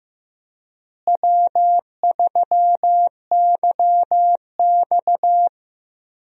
Morse code, heard as a keyed tone, about 15 wpm, 700 Hz. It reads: W3YX
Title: USA Call Signs - 4 Characters - Repeated 3x - 15wpm